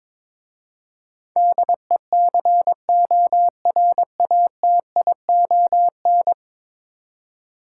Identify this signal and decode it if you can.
Morse code, heard as a keyed tone, about 22 wpm, 700 Hz. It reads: DECORATION
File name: practice / 22wpm / 10 Letter Words - First 3 Letters Given - 22wpm